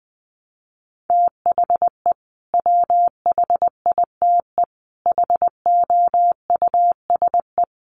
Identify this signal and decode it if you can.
Morse code, heard as a keyed tone, about 20 wpm, 700 Hz. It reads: THE WHITE HOUSE
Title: Sets of 3 Words 20wpm